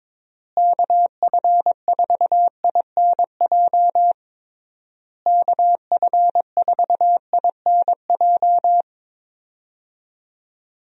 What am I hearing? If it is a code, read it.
Morse code, heard as a keyed tone, about 22 wpm, 700 Hz. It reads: KF4INJ KF4INJ
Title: USA Call Signs - 6 Characters - Repeated 3x - 22wpm